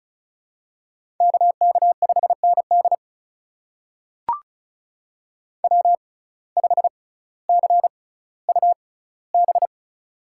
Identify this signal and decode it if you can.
Morse code, heard as a keyed tone, about 36 wpm, spaced as for 18 wpm, 700 Hz. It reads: KK5ND W5CUB
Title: USA Call Signs - 5 Characters - 6x Character Spacing - 35wpm